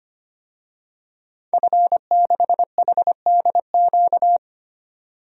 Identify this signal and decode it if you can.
Morse code, heard as a keyed tone, about 25 wpm, 700 Hz. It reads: F6HDQ
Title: DXCC-100 Call Signs - 5 Characters - 25wpm